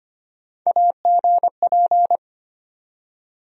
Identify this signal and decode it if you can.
Morse code, heard as a keyed tone, about 25 wpm, 700 Hz. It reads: AGP
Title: USA Call Sign Postfixes - 3 Letters 25wpm